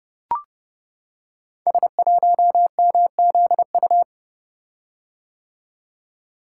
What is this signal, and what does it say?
Morse code, heard as a keyed tone, about 29 wpm, 700 Hz. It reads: S1MZU
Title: Code Groups - 5 Letter:Numbers - 30wpm